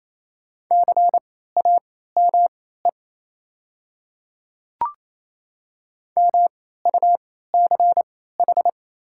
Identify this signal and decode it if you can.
Morse code, heard as a keyed tone, about 28 wpm, spaced as for 16 wpm, 700 Hz. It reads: CAME MUCH
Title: 3x Character Spacing - Top 200 Words - 28wpm